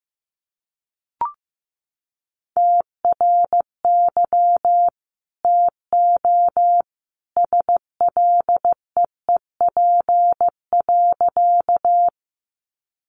Morse code, heard as a keyed tone, about 15 wpm, 700 Hz. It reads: TRY TO SLEEP.
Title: Sentences from Top 500 Words 15wpm